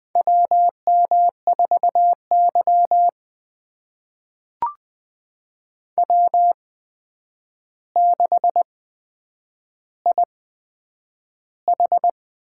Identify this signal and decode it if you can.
Morse code, heard as a keyed tone, about 20 wpm, spaced as for 9 wpm, 700 Hz. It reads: WM4Y W6IH
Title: USA Call Signs - 8x Character Spacing - 20wpm